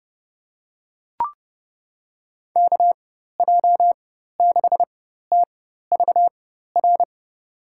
Morse code, heard as a keyed tone, about 30 wpm, spaced as for 14 wpm, 700 Hz. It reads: KJ6TVR
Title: USA Call Signs - 6 Characters - 4x Character Spacing - 30wpm